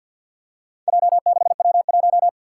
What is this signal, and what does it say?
Morse code, heard as a keyed tone, about 48 wpm, 700 Hz. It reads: J6W1
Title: Four Challenging Letter-Number Combinations 50wpm